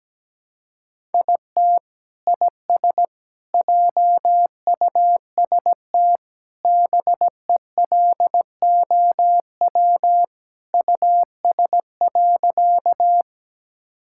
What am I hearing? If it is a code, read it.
Morse code, heard as a keyed tone, about 17 wpm, 700 Hz. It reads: IT IS JUST BELOW US.
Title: Sentences from Top 1000 Words - Review Part 5 - 17wpm